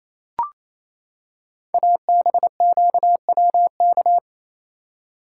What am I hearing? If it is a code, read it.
Morse code, heard as a keyed tone, about 28 wpm, 700 Hz. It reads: ABQWK
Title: Code Groups - 5 Letters - 28wpm